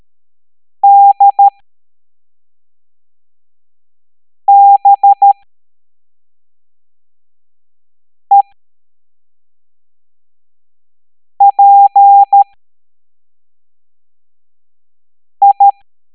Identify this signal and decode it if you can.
Morse code, heard as a keyed tone, about 13 wpm, spaced as for 3 wpm, 800 Hz. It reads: DBEPI